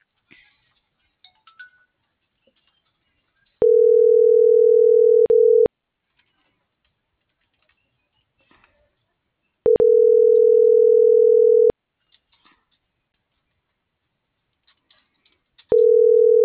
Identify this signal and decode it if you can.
Ringback tone (North America), 440 + 480 Hz.
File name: CENTRAL MULTIMODAL / out viridiana.juarez - .83254